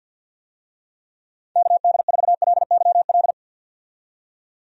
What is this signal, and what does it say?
Morse code, heard as a keyed tone, about 51 wpm, 700 Hz. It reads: KD4LYL
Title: USA Call Signs - 6 Characters - 50wpm